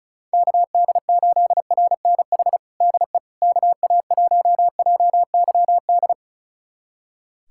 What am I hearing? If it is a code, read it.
Morse code, heard as a keyed tone, about 36 wpm, 700 Hz. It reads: KD8RNH DE KA1JYD
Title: USA Callsign DE Callsign 35wpm